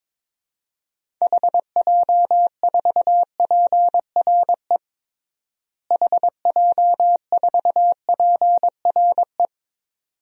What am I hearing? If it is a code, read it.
Morse code, heard as a keyed tone, about 22 wpm, 700 Hz. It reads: HJ4PRE HJ4PRE
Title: DXCC-100 Call Signs - 6 Characters - Repeated 2x - 22wpm